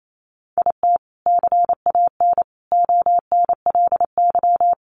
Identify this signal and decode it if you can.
Morse code, heard as a keyed tone, about 28 wpm, 700 Hz. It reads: IT CAN ONLY
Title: Sentences from Top 1400 Words 28wpm